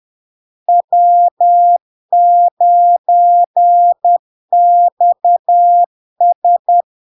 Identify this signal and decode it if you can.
Morse code, heard as a keyed tone, about 10 wpm, 700 Hz. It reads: W9XS
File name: LARK-FD-2022-Log-in-CW-10wpm-700-Hz